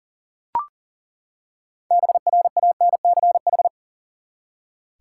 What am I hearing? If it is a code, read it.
Morse code, heard as a keyed tone, about 40 wpm, 700 Hz. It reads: BRANCH
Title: Six Letter Words - 40wpm